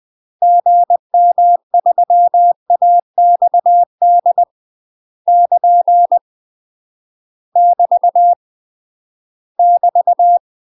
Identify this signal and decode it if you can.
Morse code, heard as a keyed tone, about 20 wpm, 700 Hz. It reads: GM3AXD <KN> <BT> <BT>
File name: UK MASTER QSOs TRAINING / 20WPM / UK MASTER QSOs 0028